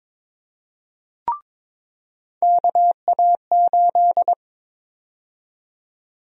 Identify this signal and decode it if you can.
Morse code, heard as a keyed tone, about 22 wpm, 700 Hz. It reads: KA8
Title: USA Call Sign Prefixes - 22wpm